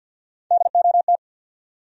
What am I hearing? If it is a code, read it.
Morse code, heard as a keyed tone, about 47 wpm, 700 Hz. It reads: DOT
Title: USA Common Names - Max 3 Letters - 50wpm